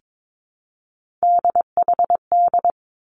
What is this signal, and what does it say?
Morse code, heard as a keyed tone, about 22 wpm, 700 Hz. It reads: DHD
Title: Three Challenging Letter-Number Combinations 22wpm